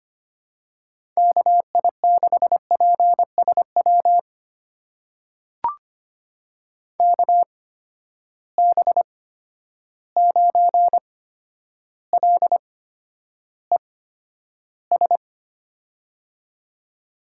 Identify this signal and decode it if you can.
Morse code, heard as a keyed tone, about 25 wpm, spaced as for 11 wpm, 700 Hz. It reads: KI6PSW KB9LES
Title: USA Call Signs - 6 Characters - 8x Character Spacing - 25wpm